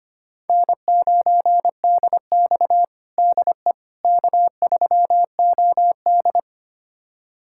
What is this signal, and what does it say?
Morse code, heard as a keyed tone, about 25 wpm, 700 Hz. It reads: N9DX DE K3OD
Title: USA Callsign DE Callsign - 4 Characters - 25wpm